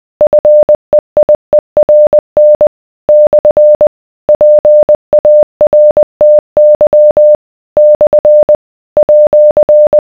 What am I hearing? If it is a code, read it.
Morse code, heard as a keyed tone, about 20 wpm, 600 Hz. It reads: FEIERN / PARTY / @